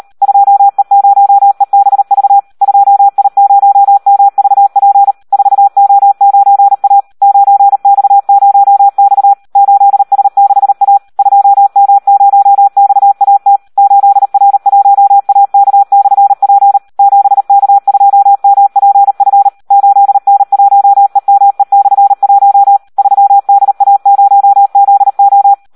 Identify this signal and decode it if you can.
Morse code, heard as a keyed tone, about 38 wpm, 800 Hz. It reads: 2E0EBV 2I0MVP 4O9A 9X0X 8S6A 2M0XAT 8R1AK/P 7K2MPF 8N1EME/1 3DA0ZO
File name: Audio practica CW / SD 1 38